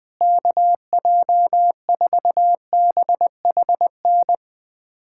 Morse code, heard as a keyed tone, about 19 wpm, 700 Hz. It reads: KJ4BHN
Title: USA Callsign DE Callsign 20wpm